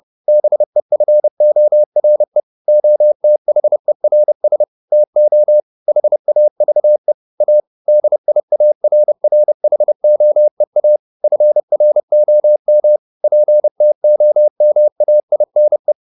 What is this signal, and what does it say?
Morse code, heard as a keyed tone, about 30 wpm, 600 Hz. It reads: BEFORE OTHERS TO HAVE A DIARRHOEA FROM PTOMAINE
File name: the old man and the sea / the old man and the sea 1 30wpm